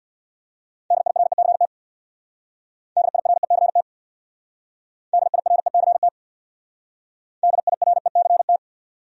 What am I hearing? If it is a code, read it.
Morse code, heard as a keyed tone, about 61 wpm, 700 Hz. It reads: DIRECT DIRECT DIRECT DIRECT
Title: Speed-Racing Top 500 Words 50wpm